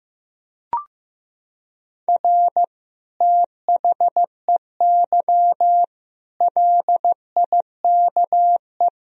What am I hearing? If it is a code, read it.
Morse code, heard as a keyed tone, about 16 wpm, 700 Hz. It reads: R THEY LIKE
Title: Sentences from Top 200 Words and Abbreviations 15wpm